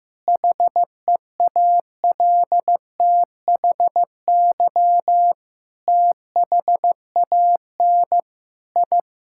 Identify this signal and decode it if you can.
Morse code, heard as a keyed tone, about 15 wpm, 700 Hz. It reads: HEALTHY THAN I
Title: Sentences from Top 1000 Words - Review Part 5 - 15wpm